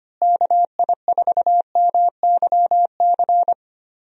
Morse code, heard as a keyed tone, about 25 wpm, 700 Hz. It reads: KI4MYC
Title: USA Call Signs - 6 Characters - Repeated 3x - 25wpm